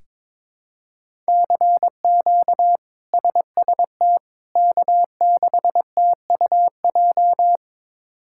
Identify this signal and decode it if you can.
Morse code, heard as a keyed tone, about 21 wpm, 700 Hz. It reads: CQ SST K6TUJ
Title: K1USN SST QSO Practice - 22wpm